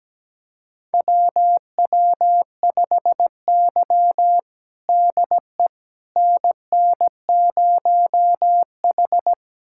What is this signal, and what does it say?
Morse code, heard as a keyed tone, about 17 wpm, 700 Hz. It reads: WW5Y DE NN0H
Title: USA Callsign DE Callsign - 4 Characters - 17wpm